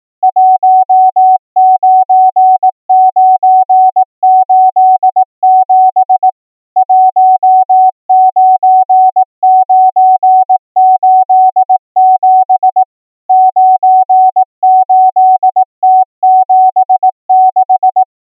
Morse code, heard as a keyed tone, about 18 wpm, 750 Hz. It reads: 19987 19987 98T76
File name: numbers / V2-morse